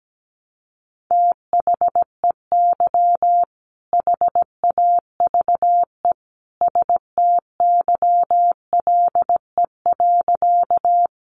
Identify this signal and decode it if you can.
Morse code, heard as a keyed tone, about 17 wpm, 700 Hz. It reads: THEY HAVE STYLE.
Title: Sentences from Top 800 Words 17wpm